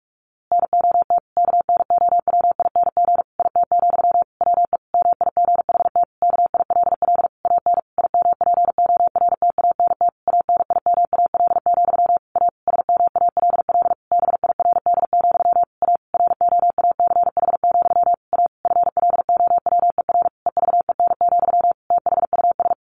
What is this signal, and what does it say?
Morse code, heard as a keyed tone, about 44 wpm, 700 Hz. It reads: NOT KNOWING IT, WE MIGHT KILL AN IMPORTANT ANIMAL, A SMALL BIRD, A ROACH, A FLOWER EVEN, THUS
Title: A Sound Of Thunder 45wpm